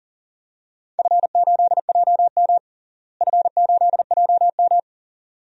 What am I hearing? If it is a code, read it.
Morse code, heard as a keyed tone, about 40 wpm, 700 Hz. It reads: F8JM F8JM
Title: DXCC-100 Call Signs - 4 Characters - Repeated 2x - 40wpm